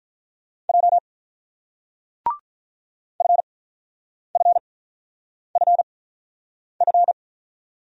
Morse code, heard as a keyed tone, about 49 wpm, spaced as for 9 wpm, 690 Hz. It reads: J FFFF